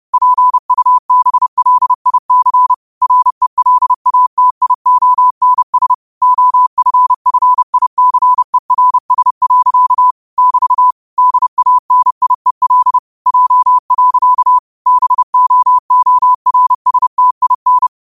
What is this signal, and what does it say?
Morse code, heard as a keyed tone, about 30 wpm, 1000 Hz. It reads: PUBLIC RELATIONS OFFICERS. <BT> DANIEL J. BOORSTIN